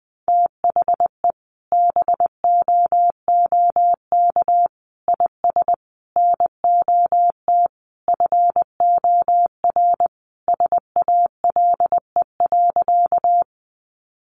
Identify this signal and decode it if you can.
Morse code, heard as a keyed tone, about 20 wpm, 700 Hz. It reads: THE BOOK IS NOT FOR SALE.
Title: Sentences from Top 1400 Words 20wpm